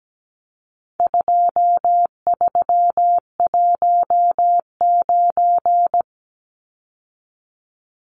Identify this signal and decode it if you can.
Morse code, heard as a keyed tone, about 17 wpm, 700 Hz. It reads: 2319
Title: Numbers - 4-Digit 17wpm